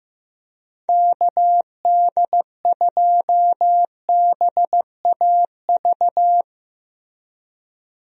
Morse code, heard as a keyed tone, about 16 wpm, 700 Hz. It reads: KD2BAV
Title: USA Call Signs - 6 Characters - Repeated 3x - 15wpm